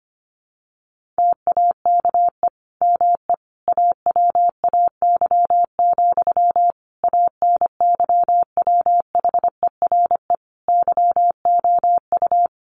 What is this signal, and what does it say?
Morse code, heard as a keyed tone, about 25 wpm, 700 Hz. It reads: TAKE ME AWAY, ANYWHERE YOU
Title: Sentences from Top 1400 Words 25wpm